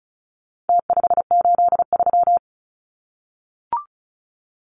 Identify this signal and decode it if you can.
Morse code, heard as a keyed tone, about 34 wpm, 700 Hz. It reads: T583
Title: Numbers - Contest Style - 4-Digit 35wpm